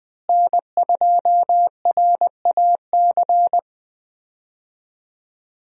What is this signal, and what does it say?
Morse code, heard as a keyed tone, about 20 wpm, 700 Hz. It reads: N2RAC
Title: USA Call Signs - 5 Characters - Repeated 3x - 20wpm